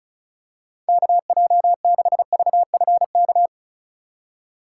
Morse code, heard as a keyed tone, about 34 wpm, 700 Hz. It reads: KJ6VFK
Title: USA Call Signs - 6 Characters - 2x Character Spacing - 35wpm